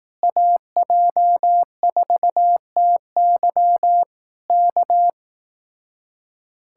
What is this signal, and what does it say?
Morse code, heard as a keyed tone, about 18 wpm, 700 Hz. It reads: AJ4TY K
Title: Speed-Racing Common QSO Phrases Encore - 15wpm